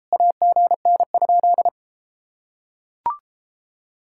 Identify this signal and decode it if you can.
Morse code, heard as a keyed tone, about 33 wpm, 700 Hz. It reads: AGN?